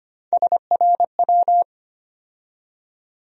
Morse code, heard as a keyed tone, about 25 wpm, 700 Hz. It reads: SRW
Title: USA Call Sign Postfixes - 3 Letters 25wpm